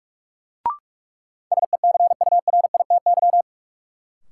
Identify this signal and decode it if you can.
Morse code, heard as a keyed tone, about 46 wpm, 700 Hz. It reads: SECURITY
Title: Sentences from Top 1400 Words - New Word Drill Alt - 45wpm